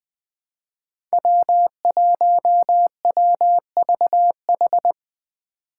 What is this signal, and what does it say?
Morse code, heard as a keyed tone, about 20 wpm, 700 Hz. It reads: W1WVH
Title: Five Challenging Letter-Number Combinations 20wpm